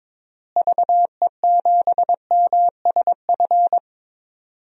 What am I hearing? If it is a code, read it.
Morse code, heard as a keyed tone, about 22 wpm, 700 Hz. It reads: VE7MSF
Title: DXCC-100 Call Signs - 6 Characters - Repeated 2x - 22wpm